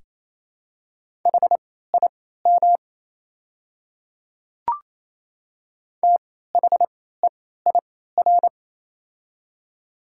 Morse code, heard as a keyed tone, about 27 wpm, spaced as for 16 wpm, 700 Hz. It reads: HIM THEIR
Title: 3x Character Spacing - Top 100 Words - 28wpm